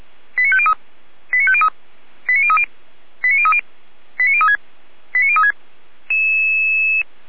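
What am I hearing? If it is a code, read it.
Five-tone selective call (ZVEI), heard as quick runs of five tones, tones of about 70 ms. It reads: 89503 89503 89030 89030 89036 89036
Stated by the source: ZVEI five-tone selective call 89099 received by SDR, from the feuerwehr-orlamuende fire brigade audio server.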